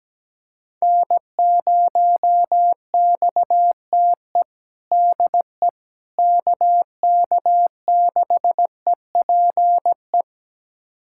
Morse code, heard as a keyed tone, about 17 wpm, 700 Hz. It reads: N0XTE DE KK6EPE
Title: USA Callsign DE Callsign 17wpm